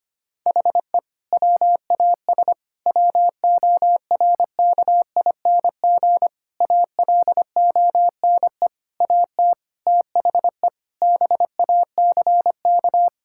Morse code, heard as a keyed tone, about 25 wpm, 700 Hz. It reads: HE WAS WORKING ALONE AT THE BACK
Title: Sentences from Top 1000 Words - Review Part 5 - 25wpm